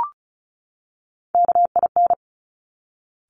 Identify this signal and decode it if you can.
Morse code, heard as a keyed tone, about 36 wpm, 700 Hz. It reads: KIN
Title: Top Three Letter Combinations in Words 35wpm